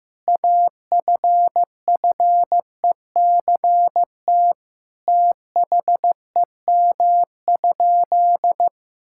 Morse code, heard as a keyed tone, about 16 wpm, 700 Hz. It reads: AFFECT THEM?
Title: Sentences from Top 1000 Words - Review Part 6 - 15wpm